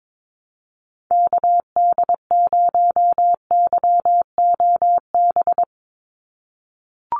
Morse code, heard as a keyed tone, about 22 wpm, 700 Hz. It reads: KD0YOB
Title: NA SOTA Activator Callsigns 22wpm